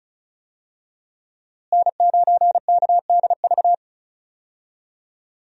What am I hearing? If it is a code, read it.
Morse code, heard as a keyed tone, about 36 wpm, 700 Hz. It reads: N9KDV
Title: USA Call Signs - 5 Characters - Repeated 2x - 35wpm